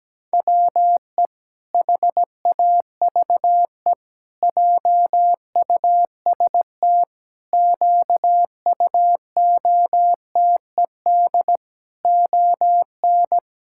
Morse code, heard as a keyed tone, about 17 wpm, 700 Hz. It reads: WE HAVE JUST QUOTED ON
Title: Pangram Challenge v2 - 17wpm